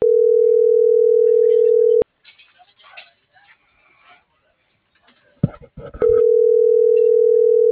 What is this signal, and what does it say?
Ringback tone (North America), 440 + 480 Hz.
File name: CENTRAL MAYAB / out lizbeth.mayab - .22483